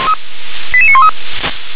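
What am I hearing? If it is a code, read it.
Five-tone selective call (ZVEI), heard as quick runs of five tones, tones of about 70 ms. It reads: 88913
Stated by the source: Five-tone ZVEI selective call 88851 received by SDR.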